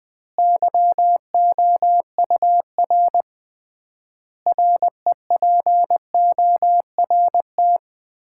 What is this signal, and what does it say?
Morse code, heard as a keyed tone, about 20 wpm, 700 Hz. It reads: YOUR REPORT
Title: Common QSO Phrases - 3x Word-Spacing - 20wpm